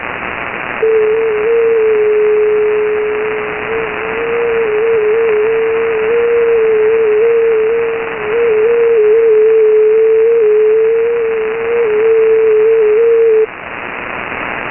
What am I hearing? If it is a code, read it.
FT8.
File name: Radio / SDR Audio / gqrx